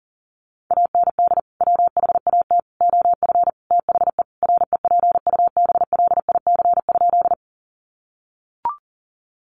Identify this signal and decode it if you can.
Morse code, heard as a keyed tone, about 40 wpm, 700 Hz. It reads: AND WHAT OF THE REPUBLIC?
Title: Sentences from Top 1500 Words 40wpm